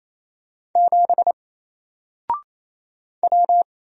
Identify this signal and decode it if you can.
Morse code, heard as a keyed tone, about 27 wpm, 700 Hz. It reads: 7 W
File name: Single-Letter-Number-28wpm